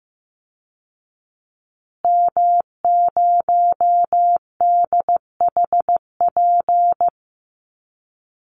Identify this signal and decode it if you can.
Morse code, heard as a keyed tone, about 15 wpm, 700 Hz. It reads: M0DHP
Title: Call Signs Repeated 2x - 15wpm